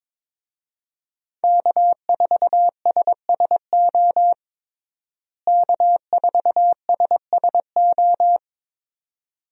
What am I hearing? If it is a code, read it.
Morse code, heard as a keyed tone, about 22 wpm, 700 Hz. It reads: K4SSO K4SSO
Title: USA Call Signs - 5 Characters - Repeated 2x - 22wpm